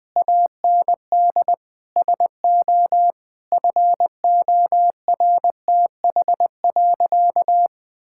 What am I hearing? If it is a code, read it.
Morse code, heard as a keyed tone, about 21 wpm, 700 Hz. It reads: AND SO FORTH.
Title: Sentences with 6-Word Phrases - 20wpm